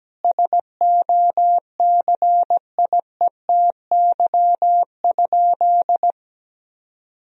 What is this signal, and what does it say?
Morse code, heard as a keyed tone, about 17 wpm, 700 Hz. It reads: SOCIETY?
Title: Sentences from Top 1000 Words - Review Part 4 - 17wpm